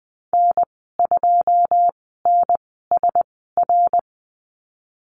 Morse code, heard as a keyed tone, about 20 wpm, spaced as for 14 wpm, 700 Hz. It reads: N2NSR
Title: 082 Lesson 14g - Number 2 - Call Signs 20wpm